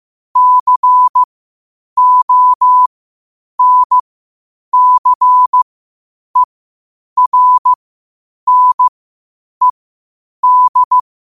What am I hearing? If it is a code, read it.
Morse code, heard as a keyed tone, about 15 wpm, spaced as for 8 wpm, 1000 Hz. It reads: CONCERNED
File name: qotdfiles / 8wpm